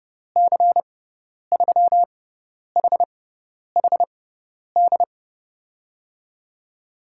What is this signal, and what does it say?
Morse code, heard as a keyed tone, about 30 wpm, spaced as for 10 wpm, 700 Hz. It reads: C3HHD
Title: USA Call Signs - 6 Characters - 6x Character Spacing - 30wpm